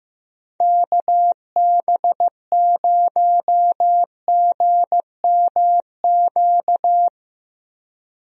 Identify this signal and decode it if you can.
Morse code, heard as a keyed tone, about 15 wpm, 700 Hz. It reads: KB0GMQ
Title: USA Call Signs - 6 Characters - 15wpm